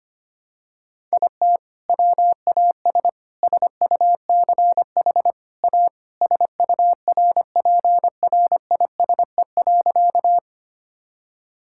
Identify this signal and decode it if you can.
Morse code, heard as a keyed tone, about 25 wpm, 700 Hz. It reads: IT WAS SUCH A SURPRISE.
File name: practice / 25wpm / Sentences with 2-Word Phrases - 25wpm